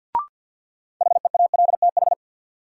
Morse code, heard as a keyed tone, about 51 wpm, 700 Hz. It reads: HEALTH
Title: Sets of 5 Words - New Word Drill - 50wpm